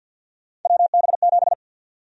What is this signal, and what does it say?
Morse code, heard as a keyed tone, about 50 wpm, 700 Hz. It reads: WB7
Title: USA Call Sign Prefixes - 50wpm